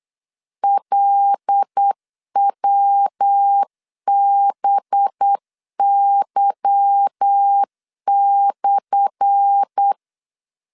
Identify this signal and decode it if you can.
Morse code, heard as a keyed tone, about 8 wpm, 800 Hz. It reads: LWBY/
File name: LEZIONI / 10 SIGNS / 10A SLASH 8wpm 3452